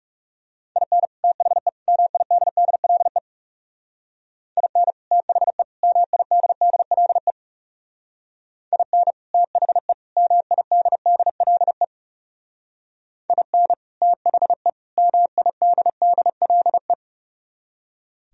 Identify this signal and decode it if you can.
Morse code, heard as a keyed tone, about 36 wpm, 700 Hz. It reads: IN THE MIDDLE IN THE MIDDLE IN THE MIDDLE IN THE MIDDLE
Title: Speed-Racing - Sets of 3 Words 30wpm